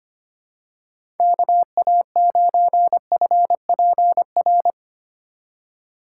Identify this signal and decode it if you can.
Morse code, heard as a keyed tone, about 25 wpm, 700 Hz. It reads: KA9FPR
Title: USA Call Signs - 6 Characters - 4x Character Spacing - 25wpm